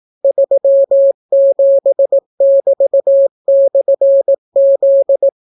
Morse code, heard as a keyed tone, about 18 wpm, 550 Hz. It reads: 37<BT>/Z